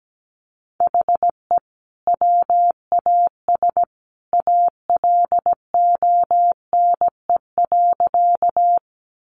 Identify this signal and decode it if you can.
Morse code, heard as a keyed tone, about 17 wpm, 700 Hz. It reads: HE WAS ALONE.
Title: Sentences from Top 1000 Words 17wpm